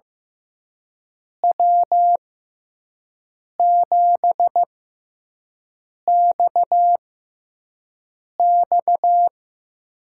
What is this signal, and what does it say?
Morse code, heard as a keyed tone, about 15 wpm, spaced as for 5 wpm, 700 Hz. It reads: W7XX
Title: USA Call Signs - 5 Characters - 6x Character Spacing - 15wpm